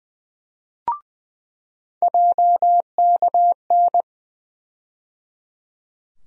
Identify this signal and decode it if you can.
Morse code, heard as a keyed tone, about 20 wpm, 700 Hz. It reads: JKN
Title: USA Call Sign Postfixes - 3 Letters 20wpm